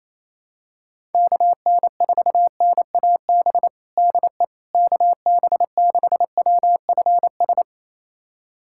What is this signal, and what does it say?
Morse code, heard as a keyed tone, about 28 wpm, 700 Hz. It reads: KN4NAB DE KB6WFS
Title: USA Callsign DE Callsign - 6 Characters - 28wpm